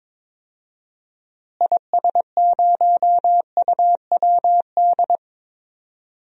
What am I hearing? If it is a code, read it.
Morse code, heard as a keyed tone, about 22 wpm, 700 Hz. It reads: IS0UWD
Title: DXCC-100 Call Signs - 6 Characters - Repeated 2x - 22wpm